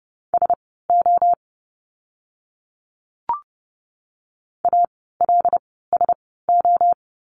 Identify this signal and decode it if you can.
Morse code, heard as a keyed tone, about 30 wpm, spaced as for 17 wpm, 700 Hz. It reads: SO ALSO
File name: course / 30:10wpm / 230 Lesson 37f - Letter x - Words 30wpm